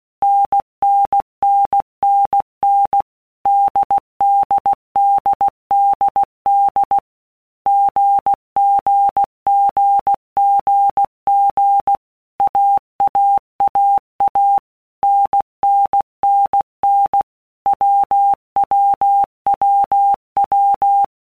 Morse code, heard as a keyed tone, about 16 wpm, 800 Hz. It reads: NNNNN DDDDD GGGGG AAAA NNNN WWWW
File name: All Groups 1-minute Blocks / Groups@15 / AWUNDG